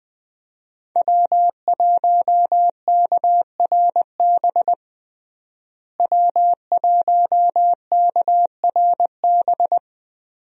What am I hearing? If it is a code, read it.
Morse code, heard as a keyed tone, about 20 wpm, 700 Hz. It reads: W1KRB W1KRB
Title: USA Call Signs - 5 Characters - Repeated 2x - 20wpm